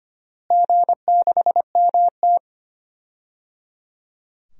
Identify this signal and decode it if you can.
Morse code, heard as a keyed tone, about 25 wpm, 700 Hz. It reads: G6MT
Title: DXCC-100 Call Signs - 4 Characters - Repeated 3x - 25wpm